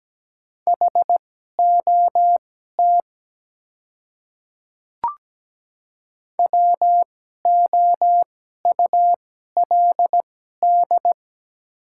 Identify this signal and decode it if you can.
Morse code, heard as a keyed tone, about 17 wpm, spaced as for 12 wpm, 700 Hz. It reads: HOT WOULD
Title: 2x Character Spacing - Top 100 Words - 17wpm